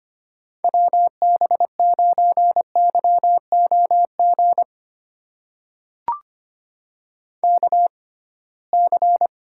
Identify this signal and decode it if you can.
Morse code, heard as a keyed tone, about 25 wpm, spaced as for 19 wpm, 700 Hz. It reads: WB9YOG KC
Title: USA Call Signs - 6 Characters - 6x Character Spacing - 25wpm